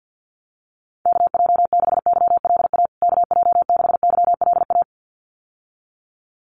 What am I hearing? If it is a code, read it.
Morse code, heard as a keyed tone, about 50 wpm, 700 Hz. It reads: KJ6YLA KJ6YLA
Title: USA Call Signs Repeated 2x 50wpm